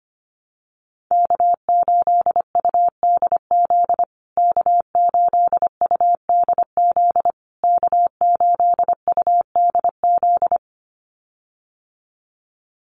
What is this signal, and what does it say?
Morse code, heard as a keyed tone, about 25 wpm, 700 Hz. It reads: K8UDZ K8UDZ K8UDZ
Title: USA Call Signs Repeated 3x 25wpm